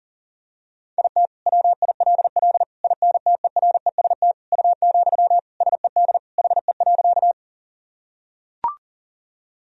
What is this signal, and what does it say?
Morse code, heard as a keyed tone, about 40 wpm, 700 Hz. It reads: IT WILL INTEREST U, SED HE.
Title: Sentences from Top 400 Words and Abbreviations 40wpm